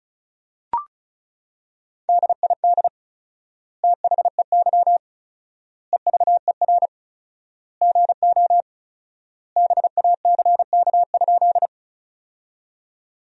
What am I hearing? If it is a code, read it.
Morse code, heard as a keyed tone, about 36 wpm, 700 Hz. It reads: DID THEY EVER GO BACK?
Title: 4x Word-Spacing - Sentences from Top 400 Words - 35wpm